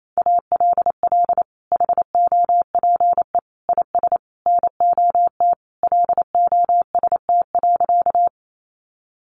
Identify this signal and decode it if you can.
Morse code, heard as a keyed tone, about 28 wpm, 700 Hz. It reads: ALL HOPE IS NOT LOST.
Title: Sentences from Top 1400 Words 28wpm